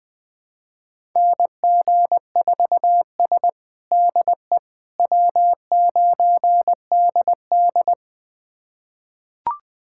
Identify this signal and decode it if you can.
Morse code, heard as a keyed tone, about 20 wpm, 700 Hz. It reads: NG4S DE W9DD
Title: USA Callsign DE Callsign - 4 Characters - 20wpm